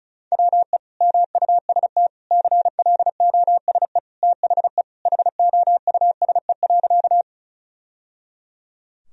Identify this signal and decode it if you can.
Morse code, heard as a keyed tone, about 34 wpm, 700 Hz. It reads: WE MUST CLOSE THE HOUSE.
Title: Sentences from Top 1000 Words - Review Part 1 - 35wpm